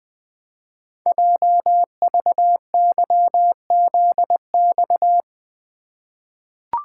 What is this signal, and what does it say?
Morse code, heard as a keyed tone, about 20 wpm, 700 Hz. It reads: JVYZX
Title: Code Groups - 5 Letters - 20wpm